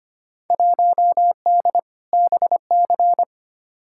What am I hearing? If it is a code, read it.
Morse code, heard as a keyed tone, about 25 wpm, 700 Hz. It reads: 1D BC
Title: Field Day Exchanges - 25wpm